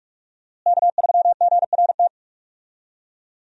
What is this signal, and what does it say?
Morse code, heard as a keyed tone, about 45 wpm, 700 Hz. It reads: K3GRT
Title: USA Call Signs - 5 Characters - Repeated 3x - 45wpm